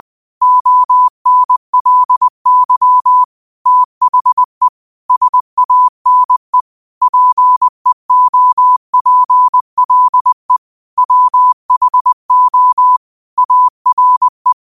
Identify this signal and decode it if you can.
Morse code, heard as a keyed tone, about 20 wpm, 1000 Hz. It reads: ONLY THE SANE PEOPLE WHO ARE
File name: qotdfiles / 20wpm